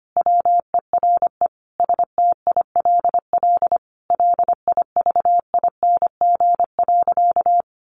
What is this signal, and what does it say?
Morse code, heard as a keyed tone, about 25 wpm, 700 Hz. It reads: WERE STILL LIVING.
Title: Sentences from Top 1100 Words 25wpm